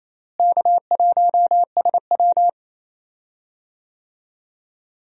Morse code, heard as a keyed tone, about 28 wpm, 700 Hz. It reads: K1SW
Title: USA Call Signs - 4 Characters - Repeated 3x - 28wpm